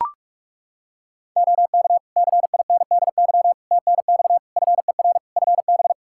Morse code, heard as a keyed tone, about 41 wpm, 700 Hz. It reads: OK CINDY TNX FER FB
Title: K1USN SST QSO Practice 45wpm